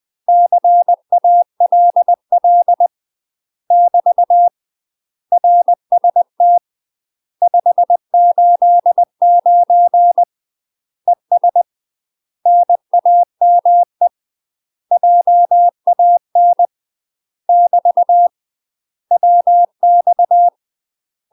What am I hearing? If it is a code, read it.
Morse code, heard as a keyed tone, about 20 wpm, 700 Hz. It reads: CALL <BT> RST 589 ES NAME JAN <BT> WX
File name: UK MASTER QSOs TRAINING / 20WPM / UK MASTER QSOs 0037